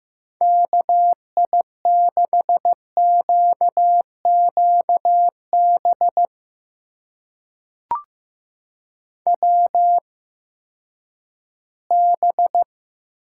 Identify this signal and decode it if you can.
Morse code, heard as a keyed tone, about 15 wpm, spaced as for 10 wpm, 700 Hz. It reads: KI6QQB WB
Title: USA Call Signs - 6 Characters - 8x Character Spacing - 15wpm